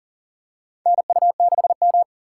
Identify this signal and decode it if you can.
Morse code, heard as a keyed tone, about 40 wpm, 700 Hz. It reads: NU6M
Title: USA Call Signs - 4 Characters - 6x Character Spacing - 40wpm